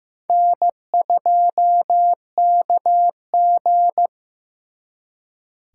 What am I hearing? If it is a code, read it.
Morse code, heard as a keyed tone, about 18 wpm, 700 Hz. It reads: N2KG
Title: Speed Racing - USA 1x2 Call Signs 15wpm